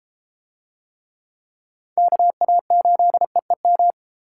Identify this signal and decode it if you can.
Morse code, heard as a keyed tone, about 34 wpm, 700 Hz. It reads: KA8EEM